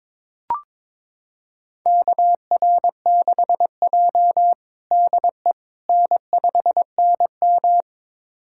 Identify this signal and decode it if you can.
Morse code, heard as a keyed tone, about 22 wpm, 700 Hz. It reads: KR6J DE N5NM
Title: USA Callsign DE Callsign - 4 Characters - 22wpm